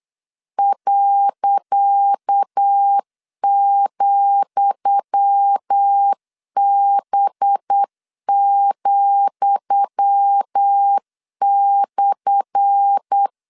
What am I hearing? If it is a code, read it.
Morse code, heard as a keyed tone, about 8 wpm, 800 Hz. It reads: .,B,/
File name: LEZIONI / 10 SIGNS / 10D COMA 8wpm 3749